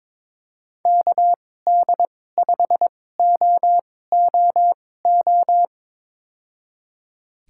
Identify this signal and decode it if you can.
Morse code, heard as a keyed tone, about 22 wpm, spaced as for 16 wpm, 700 Hz. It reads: KD5OOO
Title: USA Call Signs - 6 Characters - 2x Character Spacing - 22wpm